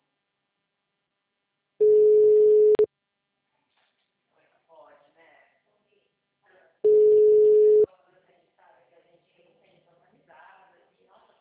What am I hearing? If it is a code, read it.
Ringback tone (Europe, 425 Hz), with speech.
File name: out- -2012- - - .10961